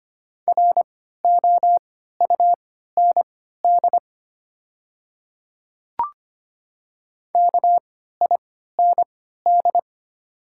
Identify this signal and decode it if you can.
Morse code, heard as a keyed tone, about 25 wpm, spaced as for 14 wpm, 700 Hz. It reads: ROUND KIND
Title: 3x Character Spacing - Top 200 Words - 25wpm